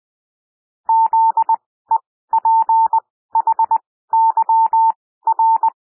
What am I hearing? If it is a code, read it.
Morse code, heard as a keyed tone, about 20 wpm, spaced as for 14 wpm, 900 Hz. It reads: ZEPHYR
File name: SOUND WORDS 20 wpm